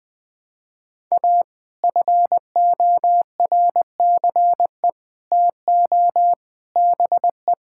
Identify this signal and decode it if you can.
Morse code, heard as a keyed tone, about 20 wpm, 700 Hz. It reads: A FORCE TO BE
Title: Sets of 8 Words - 20wpm